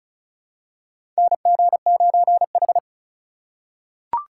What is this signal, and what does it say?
Morse code, heard as a keyed tone, about 36 wpm, 700 Hz. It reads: NG9H
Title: USA Call Signs - 3x Character Spacing - 35wpm